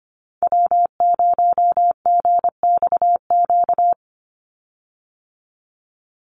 Morse code, heard as a keyed tone, about 25 wpm, 700 Hz. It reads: W0GXQ
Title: Call Signs - 25wpm